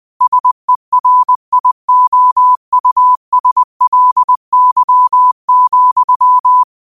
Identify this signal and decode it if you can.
Morse code, heard as a keyed tone, about 20 wpm, 1000 Hz. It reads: SERIOUSLY,
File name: qotdfiles / 20wpm